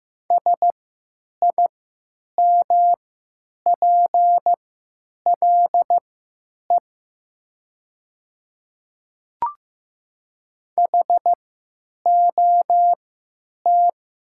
Morse code, heard as a keyed tone, about 15 wpm, spaced as for 9 wpm, 700 Hz. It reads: SIMPLE HOT
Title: 3x Character Spacing - Top 500 Words - 15wpm